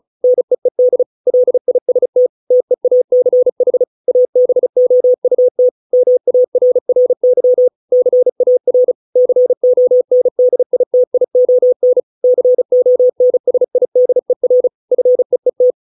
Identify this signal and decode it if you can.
Morse code, heard as a keyed tone, about 35 wpm, 500 Hz. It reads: NEED LIST TEACH ABOUT MARRY CAR CONDITION CONSIDER FEET